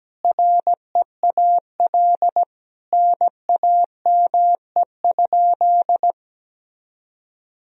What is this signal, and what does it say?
Morse code, heard as a keyed tone, about 17 wpm, 700 Hz. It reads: REAL NAME?
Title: Sentences from Top 300 Words and Abbreviations 17wpm